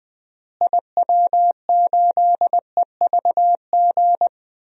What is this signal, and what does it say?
Morse code, heard as a keyed tone, about 20 wpm, 700 Hz. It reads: IW8EVG
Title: DXCC-100 Call Signs - 6 Characters - Repeated 2x - 20wpm